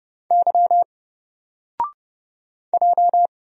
Morse code, heard as a keyed tone, about 30 wpm, 700 Hz. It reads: Y J
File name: Single-Letter-Number-30wpm